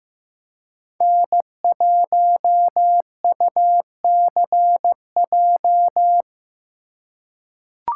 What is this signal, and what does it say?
Morse code, heard as a keyed tone, about 15 wpm, 700 Hz. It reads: N1UCJ
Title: USA Call Signs - 5 Characters - 3x Character Spacing - 15wpm